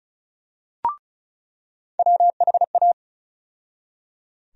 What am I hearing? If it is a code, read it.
Morse code, heard as a keyed tone, about 36 wpm, 700 Hz. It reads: WHA
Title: USA Call Sign Postfixes - 3 Letters 35wpm